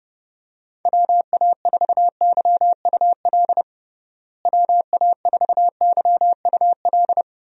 Morse code, heard as a keyed tone, about 29 wpm, 700 Hz. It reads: WA4YUL WA4YUL
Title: USA Call Signs - 6 Characters - Repeated 3x - 30wpm